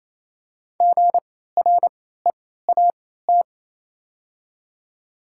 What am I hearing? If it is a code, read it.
Morse code, heard as a keyed tone, about 27 wpm, spaced as for 16 wpm, 700 Hz. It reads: GREAT
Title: 3x Character Spacing - Top 300 Words - 28wpm